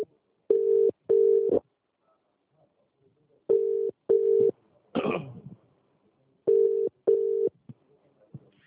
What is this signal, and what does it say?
Ringback tone (UK), 400 + 450 Hz.